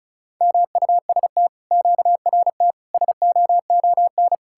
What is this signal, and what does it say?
Morse code, heard as a keyed tone, about 47 wpm, 700 Hz. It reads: MUST QRT SOON